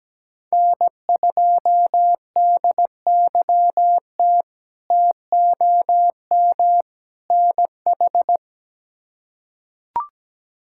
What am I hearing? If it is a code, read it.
Morse code, heard as a keyed tone, about 17 wpm, 700 Hz. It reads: N2DYT TOM NH
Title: K1USN SST QSO Practice - 17wpm